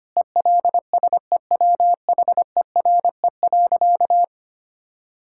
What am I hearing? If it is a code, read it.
Morse code, heard as a keyed tone, about 25 wpm, 700 Hz. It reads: ELSEWHERE.
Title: Sentences from Top 1600 Words 25wpm